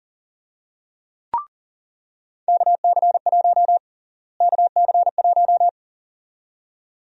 Morse code, heard as a keyed tone, about 40 wpm, 700 Hz. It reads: KC1 KC1
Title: USA Call Sign Prefixes - Repeated 2x - 40wpm